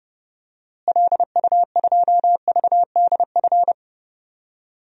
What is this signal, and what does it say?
Morse code, heard as a keyed tone, about 32 wpm, 700 Hz. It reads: LU2VDF
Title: DXCC-100 Call Signs - 6 Characters - Repeated 3x - 30wpm